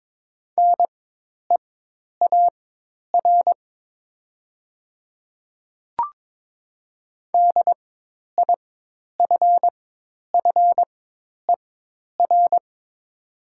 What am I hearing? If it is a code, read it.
Morse code, heard as a keyed tone, about 22 wpm, spaced as for 10 wpm, 700 Hz. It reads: NEAR DIFFER
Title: 4x Character Spacing - Top 300 Words - 22wpm